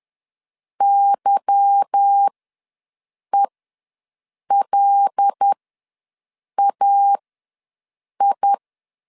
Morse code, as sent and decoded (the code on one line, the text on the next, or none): -.-- . .-.. .- ..
YELAI